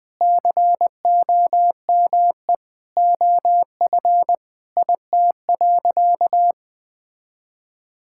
Morse code, heard as a keyed tone, about 20 wpm, 700 Hz. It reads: COME OF IT.
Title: Sentences with 2-Word Phrases - Spoken Beginning - 20wpm